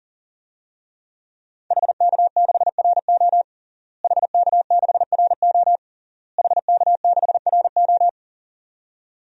Morse code, heard as a keyed tone, about 37 wpm, 700 Hz. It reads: HK6RO HK6RO HK6RO
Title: DXCC-100 Call Signs - 5 Characters - Repeated 3x - 40wpm